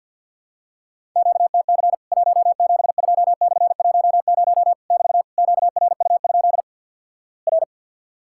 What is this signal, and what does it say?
Morse code, heard as a keyed tone, about 50 wpm, 700 Hz. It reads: QTC 172/10 <BT> QRU? R
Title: Worked All Europe DX Contest - QTC Exchange - 50wpm